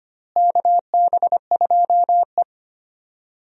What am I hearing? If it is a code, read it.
Morse code, heard as a keyed tone, about 25 wpm, 700 Hz. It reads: KB2E
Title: USA Call Signs - 4 Characters - Repeated 3x - 25wpm